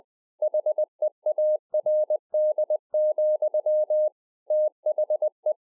Morse code, heard as a keyed tone, about 20 wpm, 610 Hz. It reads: HEARD, THE
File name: 20WPM